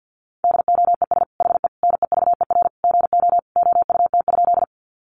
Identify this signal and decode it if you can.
Morse code, heard as a keyed tone, about 50 wpm, 700 Hz. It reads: DOES HE NEVER GO OUT?
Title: Sentences from Top 300 Words 50wpm